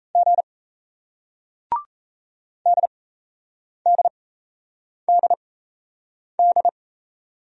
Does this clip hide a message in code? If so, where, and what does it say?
Yes: Morse code from 0.1 to 6.7 s, heard as a keyed tone, about 37 wpm, spaced as for 8 wpm, 700 Hz. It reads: G DDDD